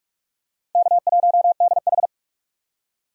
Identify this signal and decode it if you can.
Morse code, heard as a keyed tone, about 47 wpm, 700 Hz. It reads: K1DH
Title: USA Call Signs - 4 Characters - 45wpm